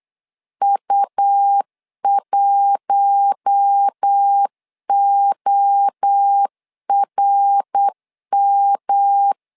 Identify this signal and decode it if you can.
Morse code, heard as a keyed tone, about 8 wpm, 800 Hz. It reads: U1ORM
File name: LEZIONI / 4 Q1 8wpm 3035